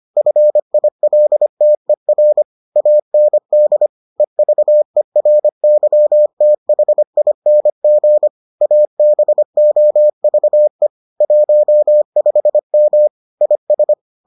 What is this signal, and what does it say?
Morse code, heard as a keyed tone, about 25 wpm, 600 Hz. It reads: FILTER AND EVERYTHING ABOVE 15M IS